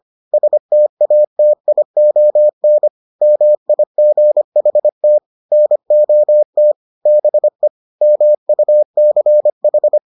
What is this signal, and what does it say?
Morse code, heard as a keyed tone, about 25 wpm, 600 Hz. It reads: STATION MIGHT NOT BE MUCH